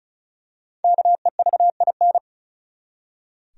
Morse code, heard as a keyed tone, about 34 wpm, 700 Hz. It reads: KEVIN
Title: USA Common Names - Max 5 Letters - 35wpm